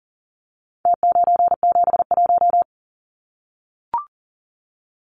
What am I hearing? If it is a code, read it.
Morse code, heard as a keyed tone, about 40 wpm, 700 Hz. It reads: T971
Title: Numbers - Contest Style - 4-Digit 40wpm